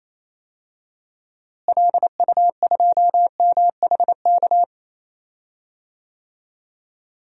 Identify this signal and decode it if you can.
Morse code, heard as a keyed tone, about 28 wpm, 700 Hz. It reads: LU2MHK
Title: DXCC-100 Call Signs - 6 Characters - Repeated 3x - 28wpm